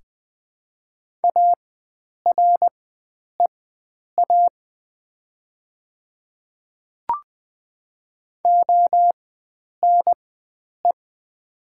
Morse code, heard as a keyed tone, about 20 wpm, spaced as for 9 wpm, 700 Hz. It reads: AREA ONE
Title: 4x Character Spacing - Top 500 Words - 20wpm